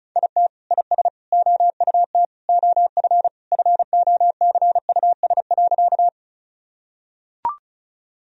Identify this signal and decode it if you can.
Morse code, heard as a keyed tone, about 34 wpm, 700 Hz. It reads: IT IS OUT OF FOCUS.
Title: Sentences from Top 1000 Words - Review Part 4 - 35wpm